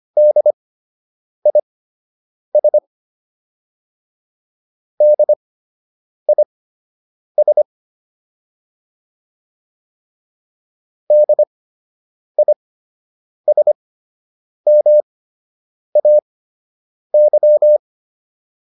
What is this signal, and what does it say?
Morse code, heard as a keyed tone, about 25 wpm, spaced as for 8 wpm, 600 Hz. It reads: DIS DIS DISMAY
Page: fundamental-practice-files